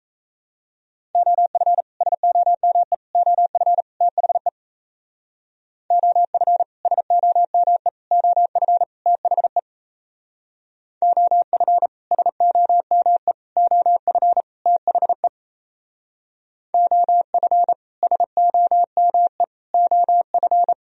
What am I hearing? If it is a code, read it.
Morse code, heard as a keyed tone, about 36 wpm, 700 Hz. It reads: OF SOME OF THE OF SOME OF THE OF SOME OF THE OF SOME OF